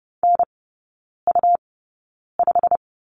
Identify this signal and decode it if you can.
Morse code, heard as a keyed tone, about 30 wpm, spaced as for 9 wpm, 700 Hz. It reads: NU5
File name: Lesson 17e - Letter u - Three Characters 30wpm